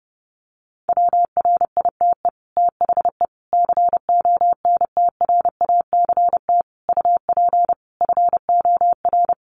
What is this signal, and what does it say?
Morse code, heard as a keyed tone, about 30 wpm, 700 Hz. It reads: WRITE THE CONTRACT UP FOR
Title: Sentences from Top 1400 Words 30wpm